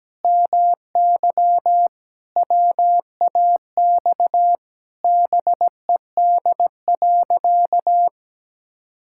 Morse code, heard as a keyed tone, about 17 wpm, 700 Hz. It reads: MY WAX BED.
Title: Pangram Challenge v2 - 17wpm